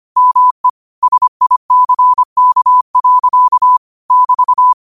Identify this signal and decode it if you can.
Morse code, heard as a keyed tone, about 25 wpm, 1000 Hz. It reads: ME SICK. <BT>